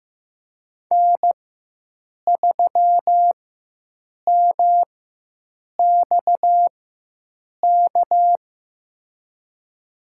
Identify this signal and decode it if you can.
Morse code, heard as a keyed tone, about 15 wpm, spaced as for 7 wpm, 700 Hz. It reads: N3MXK
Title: USA Call Signs - 5 Characters - 4x Character Spacing - 15wpm